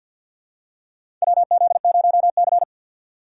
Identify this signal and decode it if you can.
Morse code, heard as a keyed tone, about 50 wpm, 700 Hz. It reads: WZ0C
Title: USA Call Signs - 4 Characters - 2x Character Spacing - 50wpm